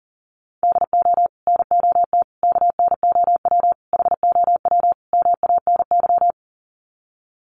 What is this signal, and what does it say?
Morse code, heard as a keyed tone, about 40 wpm, 700 Hz. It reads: DO NOT KNOW HOW MANY
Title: Sets of 5 Words Encore 40wpm